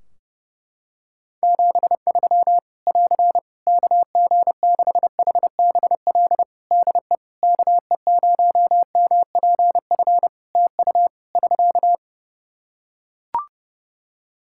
Morse code, heard as a keyed tone, about 31 wpm, 700 Hz. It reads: 73 <AR> KG6HBL DE KE0MPF TU <SK>
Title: Practice QSO - Exchange 4 - Ending - 30wpm